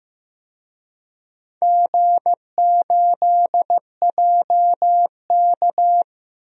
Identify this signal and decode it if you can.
Morse code, heard as a keyed tone, about 15 wpm, 700 Hz. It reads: G8JK
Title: DXCC-100 Call Signs - 4 Characters - 15wpm